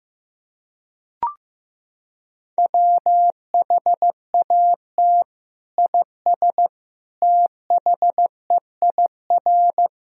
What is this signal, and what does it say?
Morse code, heard as a keyed tone, about 16 wpm, 700 Hz. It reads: WHAT IS THEIR
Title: Sentences from Top 1000 Words - Review Part 1 - 15wpm